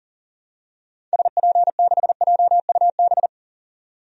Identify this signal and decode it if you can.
Morse code, heard as a keyed tone, about 40 wpm, 700 Hz. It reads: SP6JUB
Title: DXCC-100 Call Signs - 6 Characters - 40wpm